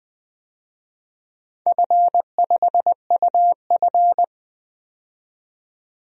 Morse code, heard as a keyed tone, about 20 wpm, 700 Hz. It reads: F5UF
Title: DXCC-100 Call Signs - 4 Characters - 20wpm